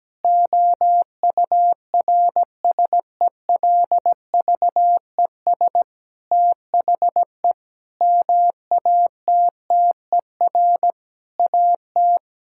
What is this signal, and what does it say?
Morse code, heard as a keyed tone, about 17 wpm, 700 Hz. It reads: OURSELVES THE MATTER AT
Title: Sentences from Top 1600 Words 17wpm